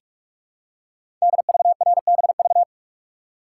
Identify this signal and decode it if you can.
Morse code, heard as a keyed tone, about 44 wpm, 700 Hz. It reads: DVRBV
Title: Code Groups - 5 Letters - 45wpm